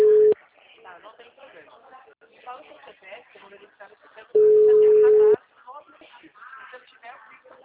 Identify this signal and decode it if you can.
Ringback tone (Europe, 425 Hz), with speech.